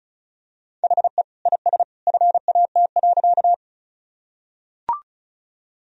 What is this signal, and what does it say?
Morse code, heard as a keyed tone, about 34 wpm, 700 Hz. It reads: HE IS FAT.
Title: Sentences from Top 1000 Words - Review Part 4 - 35wpm